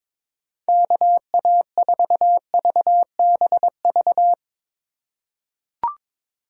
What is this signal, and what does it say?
Morse code, heard as a keyed tone, about 22 wpm, 700 Hz. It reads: KA4VBV
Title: USA Call Signs - 2x Character Spacing - 22wpm